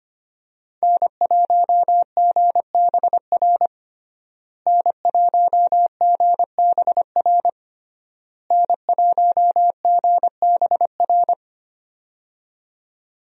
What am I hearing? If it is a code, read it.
Morse code, heard as a keyed tone, about 25 wpm, 700 Hz. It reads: N1GBR N1GBR N1GBR
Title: USA Call Signs - 5 Characters - Repeated 3x - 25wpm